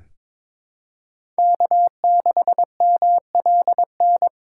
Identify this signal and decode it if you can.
Morse code, heard as a keyed tone, about 21 wpm, 700 Hz. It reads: K6MLN
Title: USA Call Signs - 5 Characters - 3x Character Spacing - 22wpm